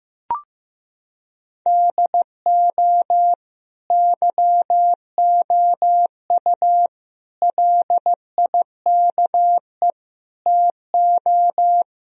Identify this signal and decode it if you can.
Morse code, heard as a keyed tone, about 16 wpm, 700 Hz. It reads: DO YOU LIKE TO
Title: Sentences from Top 1000 Words - Review Part 1 - 15wpm